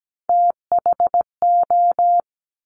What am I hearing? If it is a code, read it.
Morse code, heard as a keyed tone, about 17 wpm, 700 Hz. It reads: THO
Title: Top Three Letter Combinations in Words 17wpm